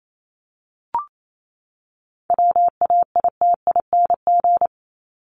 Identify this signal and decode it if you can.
Morse code, heard as a keyed tone, about 28 wpm, 700 Hz. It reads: WAITING
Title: ING Suffix - Top 100 Words with Base Words 28wpm